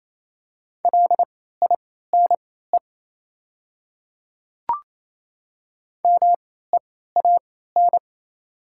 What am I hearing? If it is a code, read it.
Morse code, heard as a keyed tone, about 28 wpm, spaced as for 16 wpm, 700 Hz. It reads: LINE MEAN
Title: 3x Character Spacing - Top 200 Words - 28wpm